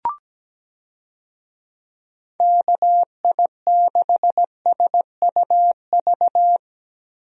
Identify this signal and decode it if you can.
Morse code, heard as a keyed tone, about 17 wpm, 700 Hz. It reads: KI6SUV
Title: USA Call Signs - 6 Characters - Repeated 2x - 17wpm